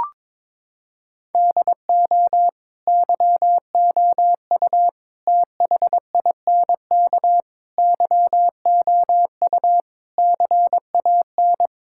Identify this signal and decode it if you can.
Morse code, heard as a keyed tone, about 22 wpm, 700 Hz. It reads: DO YOU THINK YOU CAN
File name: practice / 22wpm / Sentences with 4-Word Phrases - 22wpm